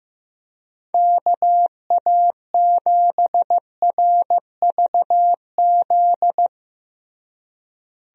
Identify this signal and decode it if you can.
Morse code, heard as a keyed tone, about 15 wpm, 700 Hz. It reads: KA7RVZ
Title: USA Call Signs - 6 Characters - Repeated 2x - 15wpm